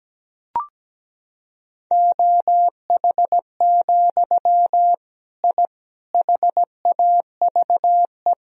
Morse code, heard as a keyed tone, about 17 wpm, 700 Hz. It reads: OH, I HAVE
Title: Sentences from Top 1000 Words - Review Part 3 - 17wpm